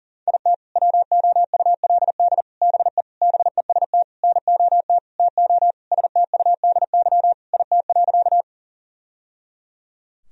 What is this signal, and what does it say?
Morse code, heard as a keyed tone, about 40 wpm, 700 Hz. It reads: IT WOULD BE BEST NOT TO STUDY IT.
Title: Sentences from Top 1000 Words - Review Part 1 - 40wpm